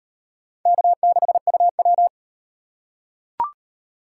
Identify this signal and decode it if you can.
Morse code, heard as a keyed tone, about 38 wpm, 700 Hz. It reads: K6UW